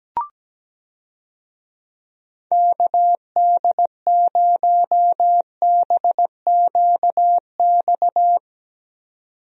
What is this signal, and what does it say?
Morse code, heard as a keyed tone, about 17 wpm, 700 Hz. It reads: KD0BQX
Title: USA Call Signs - 6 Characters - 17wpm